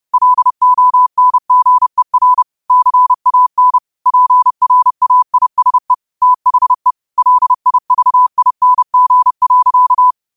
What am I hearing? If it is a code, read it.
Morse code, heard as a keyed tone, about 30 wpm, 1000 Hz. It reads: LONGER CAN PRAISE THE LIVING.